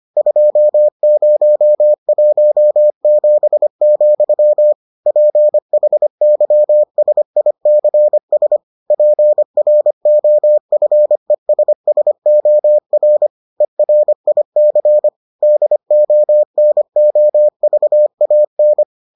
Morse code, heard as a keyed tone, about 25 wpm, 600 Hz. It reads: 2017, PHYSICS PROFESSOR ERIC DONOVAN